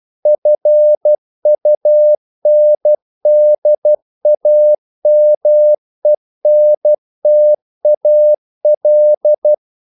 Morse code, heard as a keyed tone, about 12 wpm, 600 Hz. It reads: FUNDAMENTAL
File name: CW / Practice / EnglishQuotations0112